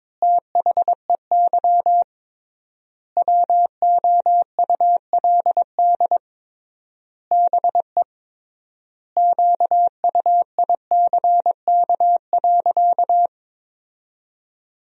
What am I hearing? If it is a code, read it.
Morse code, heard as a keyed tone, about 22 wpm, 700 Hz. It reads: THEY WOULD BE QUICK.
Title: 3x Word-Spacing - Sentences from Top 500 Words - 22wpm